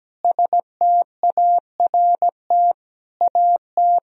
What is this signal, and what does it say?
Morse code, heard as a keyed tone, about 17 wpm, 700 Hz. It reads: START AT
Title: Sentences from Top 1000 Words - Review Part 1 - 17wpm